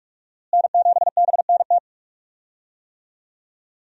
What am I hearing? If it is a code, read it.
Morse code, heard as a keyed tone, about 46 wpm, 700 Hz. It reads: N7BNT
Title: USA Call Signs - 5 Characters - Repeated 2x - 45wpm